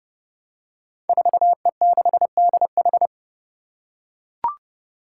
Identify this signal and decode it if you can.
Morse code, heard as a keyed tone, about 30 wpm, 700 Hz. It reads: 4E6DH
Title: Five Challenging Letter-Number Combinations 30wpm